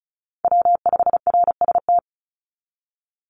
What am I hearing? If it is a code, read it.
Morse code, heard as a keyed tone, about 36 wpm, 700 Hz. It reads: W5RST
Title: NA SOTA Chaser Callsigns 35wpm